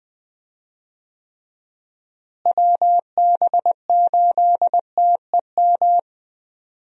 Morse code, heard as a keyed tone, about 19 wpm, 700 Hz. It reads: WB8TEM
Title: USA Call Signs - 6 Characters - Repeated 3x - 20wpm